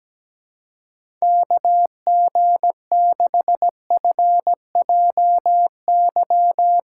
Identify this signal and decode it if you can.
Morse code, heard as a keyed tone, about 17 wpm, 700 Hz. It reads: KG6FJY
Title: USA Call Signs - 6x Character Spacing - 17wpm